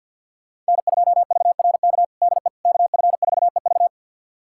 Morse code, heard as a keyed tone, about 47 wpm, 700 Hz. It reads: N2VRK DE KF4EV
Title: USA Callsign DE Callsign - 5 Characters - 50wpm